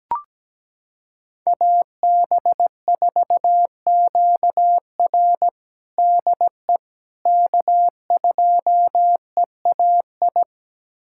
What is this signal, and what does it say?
Morse code, heard as a keyed tone, about 17 wpm, 700 Hz. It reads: AB4QR DE K2EAI
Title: USA Callsign DE Callsign - 5 Characters - 17wpm